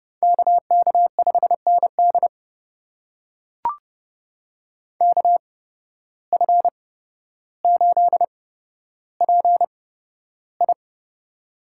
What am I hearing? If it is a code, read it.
Morse code, heard as a keyed tone, about 30 wpm, spaced as for 13 wpm, 700 Hz. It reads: KK5ND KF8PI
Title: USA Call Signs - 5 Characters - 8x Character Spacing - 30wpm